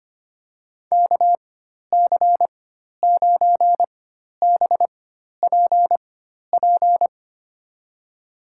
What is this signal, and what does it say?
Morse code, heard as a keyed tone, about 25 wpm, spaced as for 12 wpm, 700 Hz. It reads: KC9BPP
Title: USA Call Signs - 6 Characters - 4x Character Spacing - 25wpm